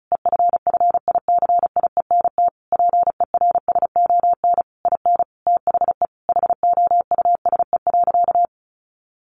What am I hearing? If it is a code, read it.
Morse code, heard as a keyed tone, about 34 wpm, 700 Hz. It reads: EFFICIENT PERSON IN THE HOUSE.
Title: Sentences from Top 1300 Words 35wpm